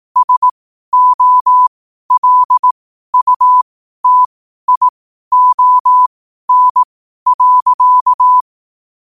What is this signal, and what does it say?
Morse code, heard as a keyed tone, about 18 wpm, spaced as for 12 wpm, 1000 Hz. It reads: SOLUTION.
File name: qotdfiles / 12wpm